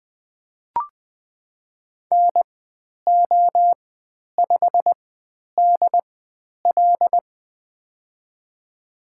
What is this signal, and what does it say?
Morse code, heard as a keyed tone, about 20 wpm, spaced as for 10 wpm, 700 Hz. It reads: NO5DL
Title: 095 Lesson 16g - Letter c - Call Signs 20wpm